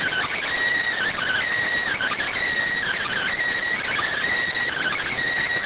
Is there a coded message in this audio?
Yes: Emergency Alert System (SAME) tones from 1.0 to 5.1 s, heard as harsh buzzing bursts.